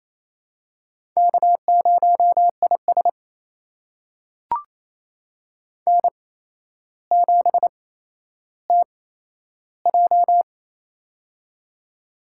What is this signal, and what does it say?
Morse code, heard as a keyed tone, about 28 wpm, spaced as for 12 wpm, 700 Hz. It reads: K0IS N7TJ
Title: USA Call Signs - 4 Characters - 8x Character Spacing - 28wpm